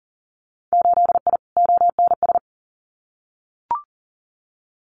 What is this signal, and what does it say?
Morse code, heard as a keyed tone, about 40 wpm, 700 Hz. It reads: 8I ONS
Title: Winter Field Day Exchanges Encore 40wpm